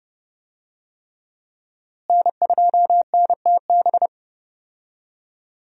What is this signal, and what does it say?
Morse code, heard as a keyed tone, about 30 wpm, 700 Hz. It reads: N2NTB
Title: USA Call Signs - 5 Characters - Repeated 3x - 30wpm